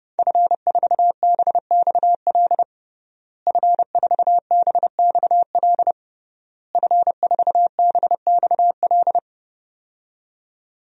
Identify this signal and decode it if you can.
Morse code, heard as a keyed tone, about 30 wpm, 700 Hz. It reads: F4BXL F4BXL F4BXL
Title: DXCC-100 Call Signs - 5 Characters - Repeated 3x - 30wpm